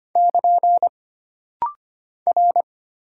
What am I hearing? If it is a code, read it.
Morse code, heard as a keyed tone, about 25 wpm, 700 Hz. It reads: <KN> R
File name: Single Character (Letter:Number:Punctuation:Prosign) - Rapid-Fire - 25wpm